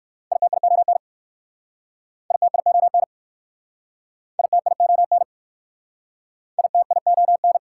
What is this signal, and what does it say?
Morse code, heard as a keyed tone, about 61 wpm, 690 Hz. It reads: ITION ITION ITION ITION